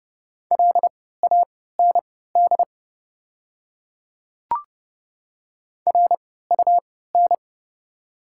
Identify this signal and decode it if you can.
Morse code, heard as a keyed tone, about 30 wpm, spaced as for 17 wpm, 690 Hz. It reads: LAND RUN
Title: 3x Character Spacing - Top 500 Words - 30wpm